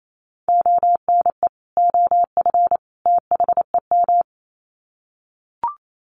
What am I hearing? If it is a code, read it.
Morse code, heard as a keyed tone, about 28 wpm, 700 Hz. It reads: ONE OF THEM
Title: Sets of 3 Words 28wpm